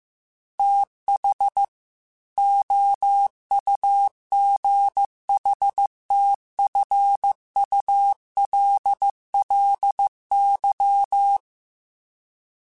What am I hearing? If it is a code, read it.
Morse code, heard as a keyed tone, about 15 wpm, 800 Hz. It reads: TH OUGHTFULLY
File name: TH Runs sp / TH Runs sp.mp3s / TH Runs sp @13